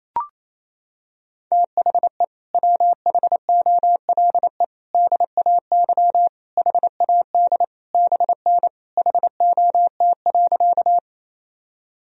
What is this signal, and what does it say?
Morse code, heard as a keyed tone, about 28 wpm, 700 Hz. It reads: THE WHOLE DAY HAD BN HOT.
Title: Sentences from Top 500 Words and Abbreviations 28wpm